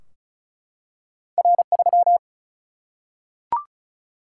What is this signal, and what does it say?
Morse code, heard as a keyed tone, about 36 wpm, 700 Hz. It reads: R3
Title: Single-Letter Single-Number 35wpm